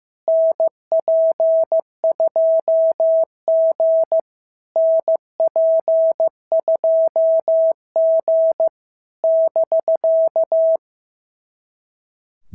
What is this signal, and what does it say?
Morse code, heard as a keyed tone, about 15 wpm, 650 Hz. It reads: NP2G NP2G <BK>
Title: SOTA QSO Practice 15wpm